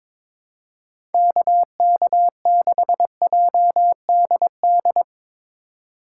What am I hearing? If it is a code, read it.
Morse code, heard as a keyed tone, about 22 wpm, 700 Hz. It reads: KK6JDD
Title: USA Call Signs - 3x Character Spacing - 22wpm